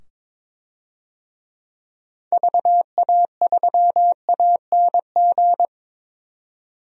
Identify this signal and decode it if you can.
Morse code, heard as a keyed tone, about 22 wpm, 700 Hz. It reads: VA3ANG
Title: DXCC-100 Call Signs - 6 Characters - 22wpm